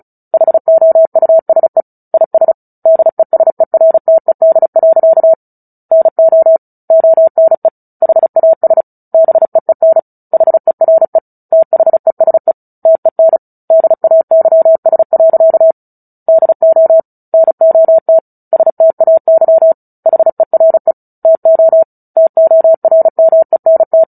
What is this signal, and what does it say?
Morse code, heard as a keyed tone, about 34 wpm, 650 Hz. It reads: HOUSE IS DESERTED. NO ONE HAS BEEN HERE THESE TEN DAYS. DO NOT STAY HERE TO TORMENT
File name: Book Two / 09-Chapter 9 - 35 WPM